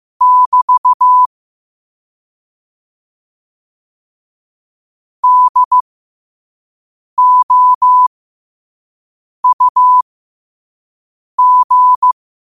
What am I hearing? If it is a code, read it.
Morse code, heard as a keyed tone, about 15 wpm, spaced as for 5 wpm, 1000 Hz. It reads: <BT> DOUG
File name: qotdfiles / 5wpm